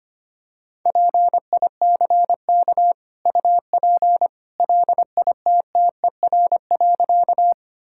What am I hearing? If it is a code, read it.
Morse code, heard as a keyed tone, about 25 wpm, 690 Hz. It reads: PICK UP LITTER.
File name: Easy Sentences with Hints - 25wpm